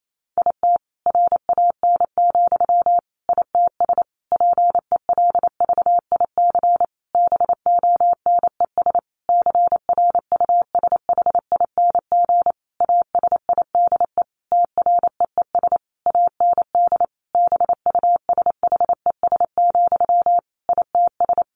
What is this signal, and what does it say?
Morse code, heard as a keyed tone, about 28 wpm, 700 Hz. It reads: IT RAN, ITS PELVIC BONES CRUSHING ASIDE TREES AND BUSHES, ITS
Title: A Sound Of Thunder 28wpm